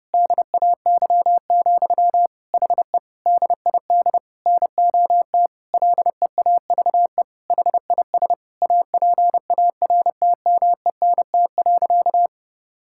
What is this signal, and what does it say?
Morse code, heard as a keyed tone, about 30 wpm, 700 Hz. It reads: DAY, HE DID NOT LEAVE HIS APARTMENT.
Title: Sentences from Top 1000 Words - Review Part 6 - 30wpm